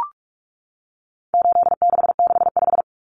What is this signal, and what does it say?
Morse code, heard as a keyed tone, about 44 wpm, 700 Hz. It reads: 8665
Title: Numbers - 4-Digit 45wpm